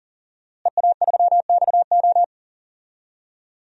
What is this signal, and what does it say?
Morse code, heard as a keyed tone, about 40 wpm, 700 Hz. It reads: EA3XO
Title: DXCC-100 Call Signs - 5 Characters - Repeated 3x - 40wpm